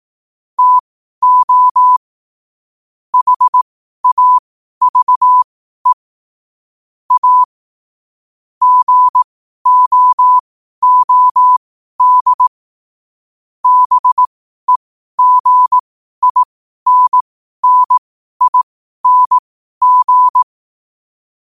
Morse code, heard as a keyed tone, about 18 wpm, spaced as for 12 wpm, 1000 Hz. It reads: TO HAVE A GOOD BEGINNING
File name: qotdfiles / 12wpm